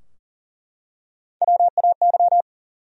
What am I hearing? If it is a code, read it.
Morse code, heard as a keyed tone, about 40 wpm, 700 Hz. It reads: WAY
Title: Top 100 Words 40wpm